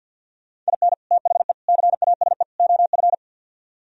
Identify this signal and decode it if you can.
Morse code, heard as a keyed tone, about 64 wpm, 700 Hz. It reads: IN THE CASE OF